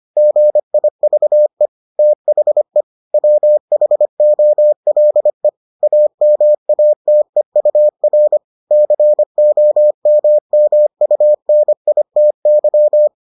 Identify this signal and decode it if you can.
Morse code, heard as a keyed tone, about 25 wpm, 600 Hz. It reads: GIVE THE WHOLE AMATEUR COMMUNITY